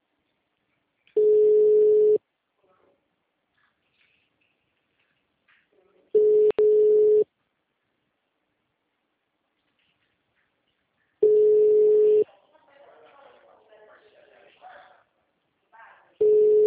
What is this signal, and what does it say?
Ringback tone (Europe, 425 Hz), with speech.
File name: out- -2015- - - .4093